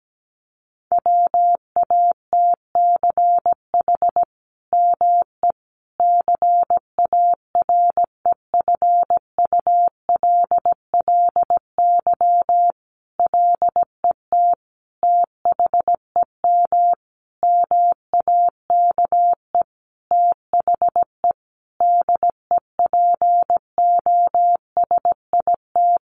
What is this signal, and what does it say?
Morse code, heard as a keyed tone, about 17 wpm, 700 Hz. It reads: WATCH ME CAREFULLY LET THEM MAKE THE DEPOSIT
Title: Sentences from Top 1300 Words 17wpm